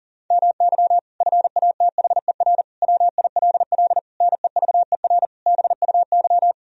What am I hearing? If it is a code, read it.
Morse code, heard as a keyed tone, about 40 wpm, 700 Hz. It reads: MY FATHER WILL NEVER BUY
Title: Sentences from Top 1000 Words - Review Part 4 - 40wpm